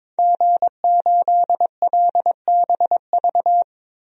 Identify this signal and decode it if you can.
Morse code, heard as a keyed tone, about 21 wpm, 700 Hz. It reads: G8LBV
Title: DXCC-100 Call Signs - 5 Characters - 22wpm